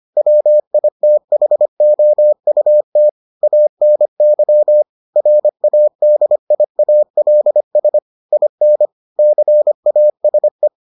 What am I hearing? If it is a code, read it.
Morse code, heard as a keyed tone, about 25 wpm, 600 Hz. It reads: WITHOUT ANY RADIALS IN CASE